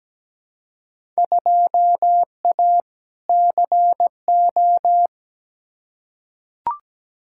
Morse code, heard as a keyed tone, about 17 wpm, 700 Hz. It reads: 2A CO
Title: Field Day Exchanges - 17wpm